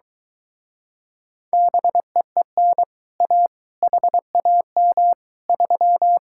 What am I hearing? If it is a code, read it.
Morse code, heard as a keyed tone, about 23 wpm, 700 Hz. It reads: BEEN A HAM 3